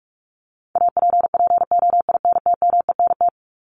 Morse code, heard as a keyed tone, about 44 wpm, 700 Hz. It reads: APPOINTMENT
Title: Top 1751-2000 Words 45wpm